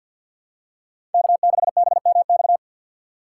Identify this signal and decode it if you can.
Morse code, heard as a keyed tone, about 50 wpm, 690 Hz. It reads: K6BMX
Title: USA Call Signs - 2x Character Spacing - 50wpm